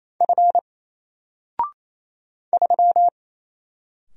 Morse code, heard as a keyed tone, about 29 wpm, 700 Hz. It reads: F 3
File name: Single-Letter-Number-28wpm